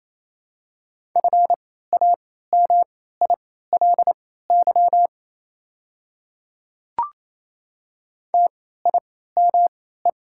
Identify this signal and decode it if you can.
Morse code, heard as a keyed tone, about 28 wpm, spaced as for 16 wpm, 700 Hz. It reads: FAMILY TIME
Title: 3x Character Spacing - Top 500 Words - 28wpm